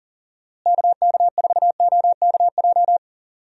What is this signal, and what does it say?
Morse code, heard as a keyed tone, about 40 wpm, 700 Hz. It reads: KK4OKJ
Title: USA Call Signs - 6 Characters - Repeated 2x - 40wpm